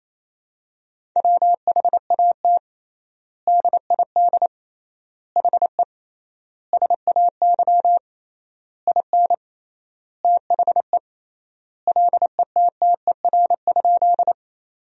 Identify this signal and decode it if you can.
Morse code, heard as a keyed tone, about 28 wpm, 700 Hz. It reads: WHAT DID HE SAY IN THE LETTER?
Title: 3x Word-Spacing - Sentences from Top 300 Words - 28wpm